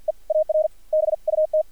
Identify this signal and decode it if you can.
Morse code, heard as a keyed tone, about 55 wpm, spaced as for 42 wpm, 620 Hz. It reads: EAR BUT
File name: archived / CRSnet